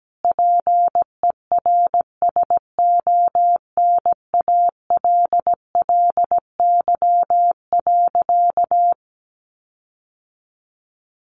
Morse code, heard as a keyed tone, about 17 wpm, 700 Hz. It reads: PERSONALLY.
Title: Sentences from Top 1400 Words 17wpm